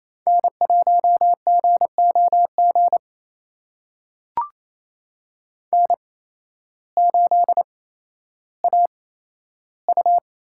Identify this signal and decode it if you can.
Morse code, heard as a keyed tone, about 28 wpm, spaced as for 13 wpm, 700 Hz. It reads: N1GOG N8AU
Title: USA Call Signs - 5 Characters - 8x Character Spacing - 28wpm